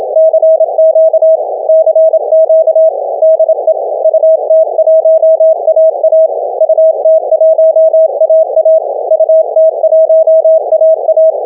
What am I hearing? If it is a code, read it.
Morse code, heard as a keyed tone, about 27 wpm, 640 Hz. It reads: CQ CQ DE UT1AA UT1AA UT1AA